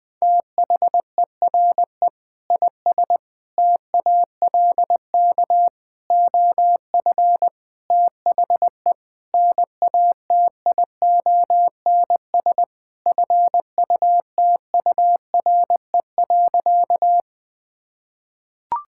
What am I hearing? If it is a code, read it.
Morse code, heard as a keyed tone, about 20 wpm, 700 Hz. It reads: THERE IS TALK OF THE NATIONS FUTURE.
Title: Sentences with 5-Word Phrases - 20wpm